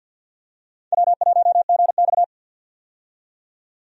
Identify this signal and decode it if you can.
Morse code, heard as a keyed tone, about 50 wpm, 700 Hz. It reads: W1GX
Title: USA Call Signs - 4 Characters - 50wpm